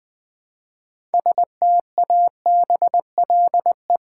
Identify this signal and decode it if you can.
Morse code, heard as a keyed tone, about 20 wpm, 700 Hz. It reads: STABLE
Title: 6 Letter Words - First 3 Letters Given - 20wpm